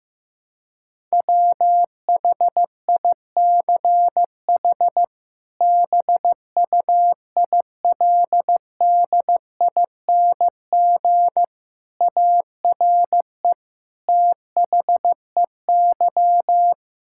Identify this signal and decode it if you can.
Morse code, heard as a keyed tone, about 15 wpm, 700 Hz. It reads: WHICH BUILDING ARE THEY
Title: Sentences from Top 1000 Words - Review Part 5 - 15wpm